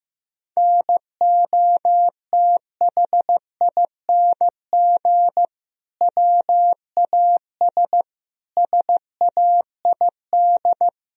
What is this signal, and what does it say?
Morse code, heard as a keyed tone, about 15 wpm, 700 Hz. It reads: NOTHING WAS SAID
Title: Sentences from Top 1000 Words - Review Part 2 - 15wpm